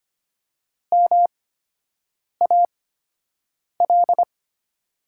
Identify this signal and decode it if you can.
Morse code, heard as a keyed tone, about 25 wpm, spaced as for 7 wpm, 690 Hz. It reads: MAL
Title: 8x Character Spacing - Top 300 Words - 25wpm